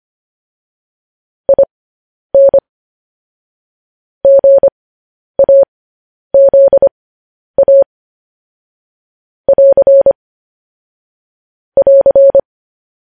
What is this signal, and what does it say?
Morse code, heard as a keyed tone, about 25 wpm, spaced as for 10 wpm, 550 Hz. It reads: IN GAZA <AR> <AR>
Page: news headlines in morse code at 10 wpm